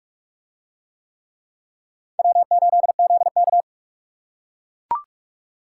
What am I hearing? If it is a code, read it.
Morse code, heard as a keyed tone, about 39 wpm, 700 Hz. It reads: W8ZK